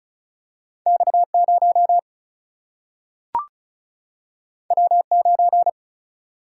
Morse code, heard as a keyed tone, about 36 wpm, 700 Hz. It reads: X0 W9
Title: Single-Letter Single-Number 35wpm